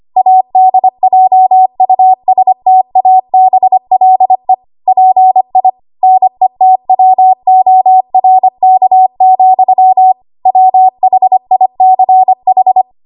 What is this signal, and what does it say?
Morse code, heard as a keyed tone, about 25 wpm, 750 Hz. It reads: ADJUSTABLE PI NETWORK, WHICH
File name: Archive / 25 WPM / 25WPM